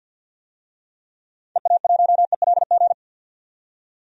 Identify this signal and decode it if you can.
Morse code, heard as a keyed tone, about 45 wpm, 700 Hz. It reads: EA1ELG
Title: DXCC-100 Call Signs - 6 Characters - 50wpm